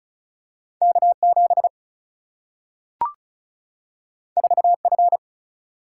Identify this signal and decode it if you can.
Morse code, heard as a keyed tone, about 36 wpm, 700 Hz. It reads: K7 4F
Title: Single-Letter Single-Number 35wpm